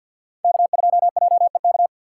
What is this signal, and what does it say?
Morse code, heard as a keyed tone, about 52 wpm, 700 Hz. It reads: K2JEK
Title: USA Call Signs - 6x Character Spacing - 50wpm